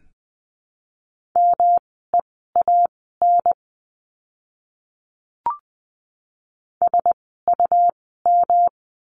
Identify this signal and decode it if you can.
Morse code, heard as a keyed tone, about 20 wpm, spaced as for 14 wpm, 700 Hz. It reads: MEAN SUM
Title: 108 Lesson 18f - Letter m - Words 20wpm